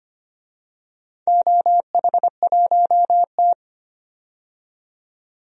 Morse code, heard as a keyed tone, about 25 wpm, 700 Hz. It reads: OH1T
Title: DXCC-100 Call Signs - 4 Characters - 25wpm